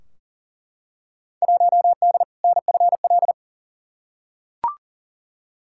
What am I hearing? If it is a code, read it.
Morse code, heard as a keyed tone, about 40 wpm, 700 Hz. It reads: 1D NFL
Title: Field Day Exchanges - 40wpm